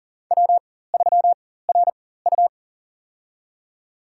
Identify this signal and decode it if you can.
Morse code, heard as a keyed tone, about 40 wpm, spaced as for 19 wpm, 700 Hz. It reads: W3RU
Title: USA Call Signs - 4 Characters - 4x Character Spacing - 40wpm